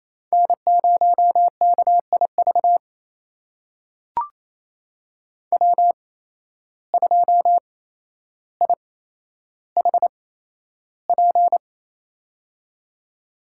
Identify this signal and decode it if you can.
Morse code, heard as a keyed tone, about 27 wpm, spaced as for 12 wpm, 700 Hz. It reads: N0KIV W2IHP
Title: USA Call Signs - 5 Characters - 8x Character Spacing - 28wpm